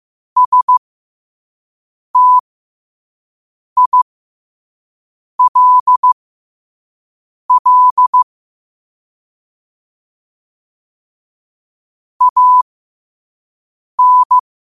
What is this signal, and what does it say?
Morse code, heard as a keyed tone, about 15 wpm, spaced as for 5 wpm, 1000 Hz. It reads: STILL AN
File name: qotdfiles / 5wpm